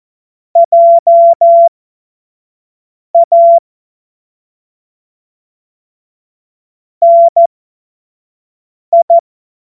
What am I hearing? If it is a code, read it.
Morse code, heard as a keyed tone, about 14 wpm, spaced as for 5 wpm, 680 Hz. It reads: JA NI
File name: Morsekurs / Witze / Witz 05. Oktober 2021, 25 F70